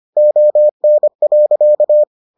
Morse code, heard as a keyed tone, about 25 wpm, 600 Hz. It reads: ON.